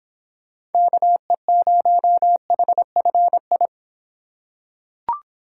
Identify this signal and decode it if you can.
Morse code, heard as a keyed tone, about 24 wpm, 700 Hz. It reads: KE0HFI